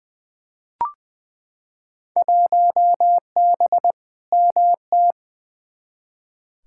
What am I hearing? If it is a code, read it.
Morse code, heard as a keyed tone, about 20 wpm, 700 Hz. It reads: 1B MT
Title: Field Day Exchanges - 20wpm